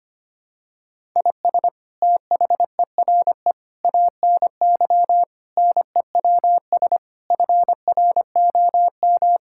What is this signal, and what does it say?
Morse code, heard as a keyed tone, about 25 wpm, 700 Hz. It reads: IS THERE ANY NEWS FROM
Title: Sentences from Top 1000 Words - Review Part 5 - 25wpm